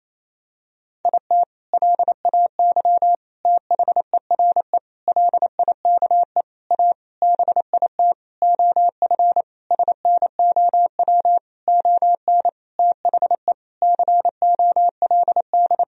Morse code, heard as a keyed tone, about 28 wpm, 700 Hz. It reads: IT LAY THERE LIKE A BIT OF SNOW ON THE COLD
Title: Sentences from Top 1000 Words - Review Part 3 - 28wpm